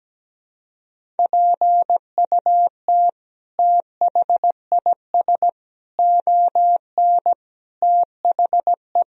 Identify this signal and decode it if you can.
Morse code, heard as a keyed tone, about 17 wpm, 700 Hz. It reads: PUT THIS ON THE
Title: Sentences from Top 1000 Words - Review Part 1 - 17wpm